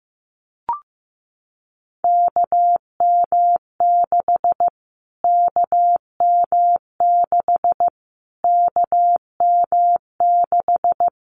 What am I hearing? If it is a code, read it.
Morse code, heard as a keyed tone, about 15 wpm, 700 Hz. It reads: KM6 KM6 KM6
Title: Callsign Prefixes for Top 25 DXCC - Repeated 3x - 15wpm